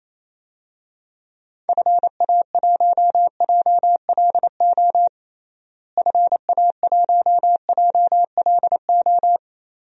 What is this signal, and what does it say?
Morse code, heard as a keyed tone, about 28 wpm, 700 Hz. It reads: FA1JLO FA1JLO
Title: DXCC-100 Call Signs - 6 Characters - Repeated 2x - 28wpm